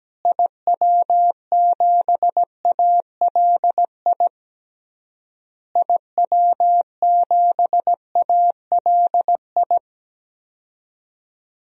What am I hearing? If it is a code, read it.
Morse code, heard as a keyed tone, about 17 wpm, 700 Hz. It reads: IW7ALI IW7ALI
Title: DXCC-100 Call Signs - 6 Characters - Repeated 3x - 17wpm